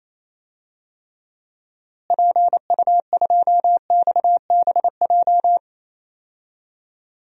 Morse code, heard as a keyed tone, about 28 wpm, 700 Hz. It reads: PU2XBJ
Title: DXCC-100 Call Signs - 6 Characters - Repeated 2x - 28wpm